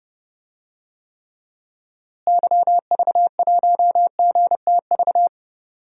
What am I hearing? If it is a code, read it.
Morse code, heard as a keyed tone, about 30 wpm, 700 Hz. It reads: YV1GTV
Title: DXCC-100 Call Signs - 6 Characters - Repeated 2x - 30wpm